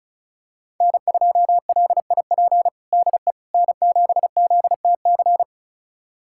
Morse code, heard as a keyed tone, about 34 wpm, 700 Hz. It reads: N2LIP DE N7ZTC
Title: USA Callsign DE Callsign - 5 Characters - 35wpm